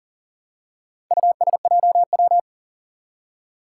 Morse code, heard as a keyed tone, about 40 wpm, 700 Hz. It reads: USJW
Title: Four Challenging Letter-Number Combinations 40wpm